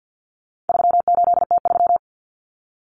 Morse code, heard as a keyed tone, about 50 wpm, 700 Hz. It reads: 38T3
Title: Common Cut Numbers - 4-Digit 50wpm